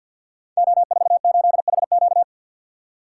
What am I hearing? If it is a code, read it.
Morse code, heard as a keyed tone, about 48 wpm, 700 Hz. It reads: O48HQ
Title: Code Groups - 5 Letter:Numbers - 50wpm